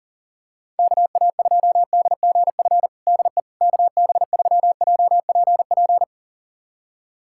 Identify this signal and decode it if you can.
Morse code, heard as a keyed tone, about 40 wpm, 700 Hz. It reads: KA2DGF DE KB3JPP
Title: USA Callsign DE Callsign - 6 Characters - 40wpm